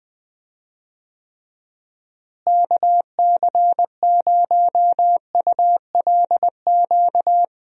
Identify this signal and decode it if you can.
Morse code, heard as a keyed tone, about 21 wpm, 700 Hz. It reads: KC0ULQ
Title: USA Call Signs - 6 Characters - Repeated 3x - 20wpm